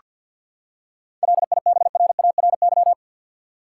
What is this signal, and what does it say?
Morse code, heard as a keyed tone, about 49 wpm, 700 Hz. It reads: LIBRARY
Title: Seven Letter Words - 50wpm